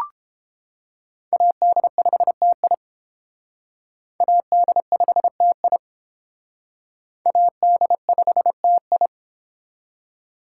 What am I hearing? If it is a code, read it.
Morse code, heard as a keyed tone, about 32 wpm, 700 Hz. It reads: AD5TI AD5TI AD5TI